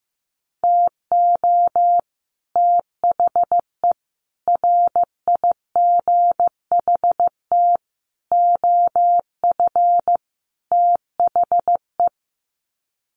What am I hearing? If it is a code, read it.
Morse code, heard as a keyed tone, about 15 wpm, 700 Hz. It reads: TO THE RIGHT OF THE
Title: Sets of 5 Words Encore 15wpm